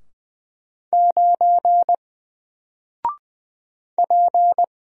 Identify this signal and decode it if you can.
Morse code, heard as a keyed tone, about 20 wpm, 700 Hz. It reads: 9 P
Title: Single Character (Letter:Number:Punctuation:Prosign) 20wpm